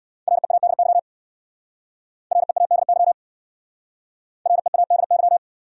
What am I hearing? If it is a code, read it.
Morse code, heard as a keyed tone, about 65 wpm, 700 Hz. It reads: READY READY READY